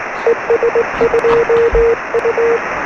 Morse code, heard as a keyed tone, about 19 wpm, 470 Hz, with radio noise. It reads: ES2U
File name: sm7lcb / 0450 es2u 13cm